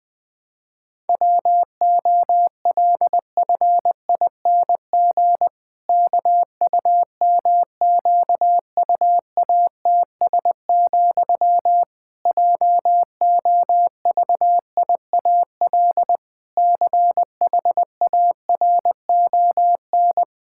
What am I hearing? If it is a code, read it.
Morse code, heard as a keyed tone, about 21 wpm, 700 Hz. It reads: WOLFING KUMQUATS, JOVIAL CHARON
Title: Pangram Challenge v2 - 20wpm